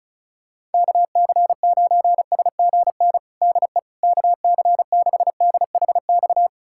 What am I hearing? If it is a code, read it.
Morse code, heard as a keyed tone, about 36 wpm, 700 Hz. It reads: KC9SGN DE KC6DHX
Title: USA Callsign DE Callsign 35wpm